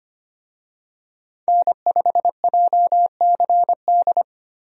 Morse code, heard as a keyed tone, about 25 wpm, 700 Hz. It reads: N5JCD
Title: USA Call Signs - 5 Characters - Repeated 2x - 25wpm